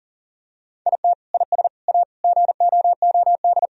Morse code, heard as a keyed tone, about 40 wpm, 700 Hz. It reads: IT IS A GOOD
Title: Sentences from Top 1000 Words - Review Part 4 - 40wpm